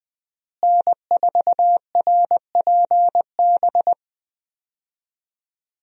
Morse code, heard as a keyed tone, about 20 wpm, 700 Hz. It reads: N4RPB
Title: USA Call Signs - 5 Characters - Repeated 3x - 20wpm